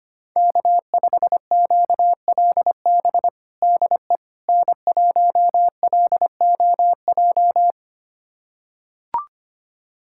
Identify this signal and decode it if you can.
Morse code, heard as a keyed tone, about 25 wpm, 700 Hz. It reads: K5QLB DE N1LOJ
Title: USA Callsign DE Callsign - 5 Characters - 25wpm